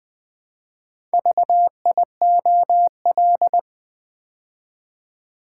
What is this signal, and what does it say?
Morse code, heard as a keyed tone, about 20 wpm, 700 Hz. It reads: VIOL
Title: Build A Word - 2 Letters At A Time - 8-Letter Words - 20wpm